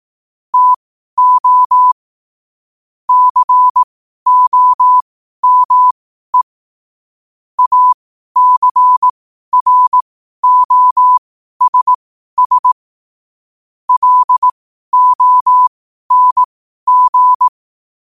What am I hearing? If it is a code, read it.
Morse code, heard as a keyed tone, about 18 wpm, spaced as for 12 wpm, 1000 Hz. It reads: TO COME ACROSS LONG
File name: qotdfiles / 12wpm